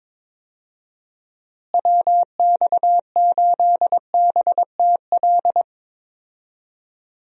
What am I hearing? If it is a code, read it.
Morse code, heard as a keyed tone, about 22 wpm, 700 Hz. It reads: WX8BTL
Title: USA Call Signs - 6 Characters - Repeated 3x - 22wpm